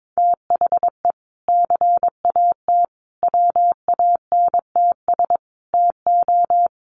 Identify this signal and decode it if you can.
Morse code, heard as a keyed tone, about 22 wpm, 700 Hz. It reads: THE CAT WANTS TO
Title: Additional Easy Phrases 22wpm